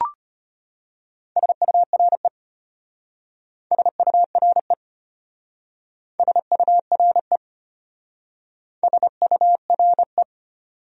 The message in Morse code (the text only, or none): SURE SURE SURE SURE